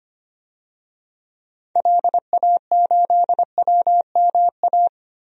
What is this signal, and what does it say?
Morse code, heard as a keyed tone, about 25 wpm, 700 Hz. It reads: LA8WMA
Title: DXCC-100 Call Signs - 6 Characters - 25wpm